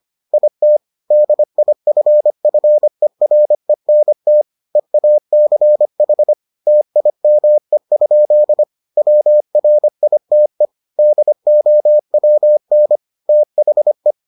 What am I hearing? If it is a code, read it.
Morse code, heard as a keyed tone, about 25 wpm, 600 Hz. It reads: IT DIFFERENT EACH TIME? WRITE DOWN THE